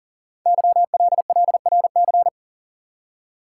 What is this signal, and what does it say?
Morse code, heard as a keyed tone, about 38 wpm, 690 Hz. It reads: YLLRC
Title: Code Groups - 5 Letters - 40wpm